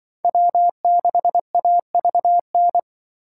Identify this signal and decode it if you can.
Morse code, heard as a keyed tone, about 25 wpm, 700 Hz. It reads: W6AVN
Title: Speed Racing - USA 1x3 Call Signs 20wpm